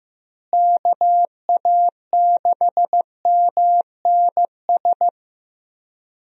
Morse code, heard as a keyed tone, about 15 wpm, 700 Hz. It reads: KA6MNS
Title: USA Call Signs - 6 Characters - Repeated 3x - 15wpm